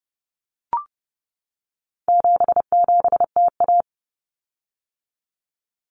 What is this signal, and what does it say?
Morse code, heard as a keyed tone, about 30 wpm, 700 Hz. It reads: 77TA
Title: Common Cut Numbers - 4-Digit 30wpm